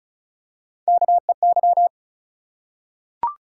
Morse code, heard as a keyed tone, about 36 wpm, 700 Hz. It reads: KEY
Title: Three Letter Words - 35wpm